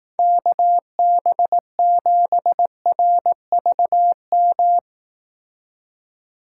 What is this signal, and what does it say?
Morse code, heard as a keyed tone, about 18 wpm, 700 Hz. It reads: KB7RVM
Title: Speed Racing - USA 2x3 Call Signs 15wpm